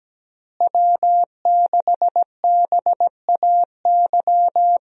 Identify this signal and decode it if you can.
Morse code, heard as a keyed tone, about 17 wpm, 700 Hz. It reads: W6BAY
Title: USA Call Signs - 5 Characters - Repeated 2x - 17wpm